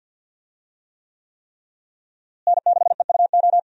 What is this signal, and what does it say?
Morse code, heard as a keyed tone, about 51 wpm, 700 Hz. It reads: N6EUO
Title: USA Call Signs - 5 Characters - Repeated 3x - 50wpm